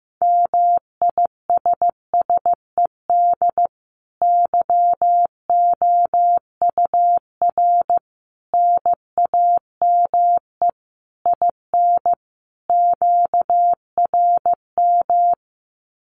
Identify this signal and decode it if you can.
Morse code, heard as a keyed tone, about 14 wpm, 700 Hz. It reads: MISSED YOUR NAME IN QRM
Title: Common QSO Phrases 15wpm